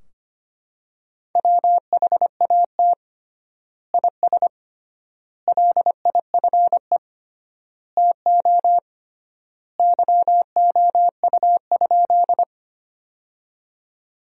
Morse code, heard as a keyed tone, about 25 wpm, 700 Hz. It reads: WHAT IS LIFE TO YOU?
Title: 3x Word-Spacing - Sentences from Top 300 Words - 25wpm